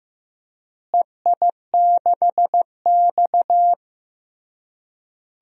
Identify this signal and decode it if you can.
Morse code, heard as a keyed tone, about 15 wpm, 700 Hz. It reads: EI6X
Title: DXCC-100 Call Signs - 4 Characters - Repeated 3x - 15wpm